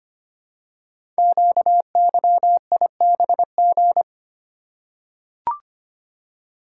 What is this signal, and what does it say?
Morse code, heard as a keyed tone, about 25 wpm, 700 Hz. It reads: QYIBG
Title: Code Groups - 5 Letters - 25wpm